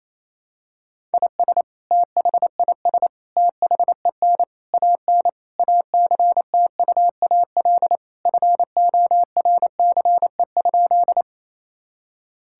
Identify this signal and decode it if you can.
Morse code, heard as a keyed tone, about 28 wpm, 700 Hz. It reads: IS THIS THEN AN ACTUAL FORCE?
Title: Sentences from Top 1000 Words - Review Part 5 - 28wpm